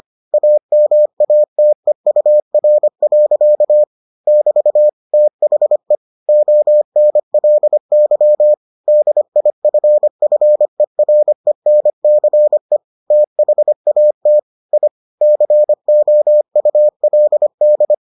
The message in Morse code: AMATEUR. <BT> THE ONLY DIFFERENCE THAT I COULD

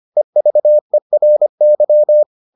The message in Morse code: EVERY

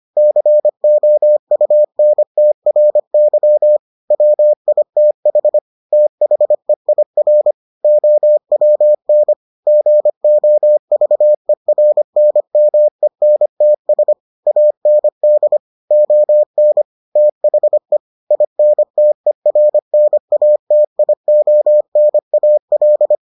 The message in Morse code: COUNTRY WITH THEIR OWN GOVERNMENTS AND ON THE INTERNATIONAL

